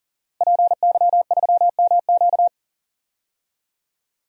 Morse code, heard as a keyed tone, about 40 wpm, 700 Hz. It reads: PY3MQ
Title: DXCC-100 Call Signs - 5 Characters - Repeated 2x - 40wpm